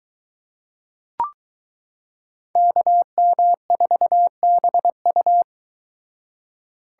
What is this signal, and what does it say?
Morse code, heard as a keyed tone, about 22 wpm, 700 Hz. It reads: KM4BU